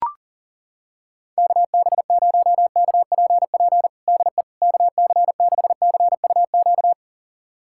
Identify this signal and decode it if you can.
Morse code, heard as a keyed tone, about 37 wpm, 700 Hz. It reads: KB0KPP DE KC6CUQ
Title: USA Callsign DE Callsign 40wpm